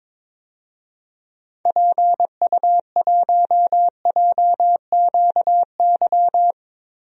Morse code, heard as a keyed tone, about 22 wpm, 700 Hz. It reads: PU1JQY
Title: DXCC-100 Call Signs - 6 Characters - Repeated 3x - 22wpm